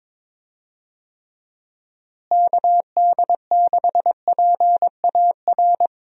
Morse code, heard as a keyed tone, about 22 wpm, 700 Hz. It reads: KD6PAR
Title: USA Call Signs - 6 Characters - Repeated 3x - 22wpm